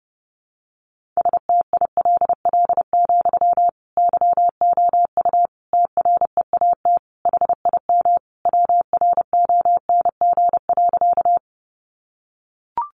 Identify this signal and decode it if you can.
Morse code, heard as a keyed tone, about 30 wpm, 700 Hz. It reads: STILL, YOU TREAT HIM WRONG.
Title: Sentences from Top 1300 Words 30wpm